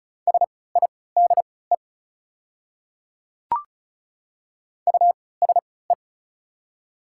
Morse code, heard as a keyed tone, about 34 wpm, spaced as for 20 wpm, 690 Hz. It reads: SIDE USE
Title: 3x Character Spacing - Top 100 Words - 35wpm